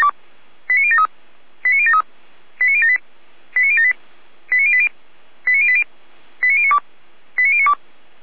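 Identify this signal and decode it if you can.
Five-tone selective call (ZVEI), heard as quick runs of five tones, tones of about 80 ms. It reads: 89073 89073 89078 89078 89080 89080 89092 89092
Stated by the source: ZVEI five-tone call received via SDR, from the Feuerwehr Orlamünde audio server.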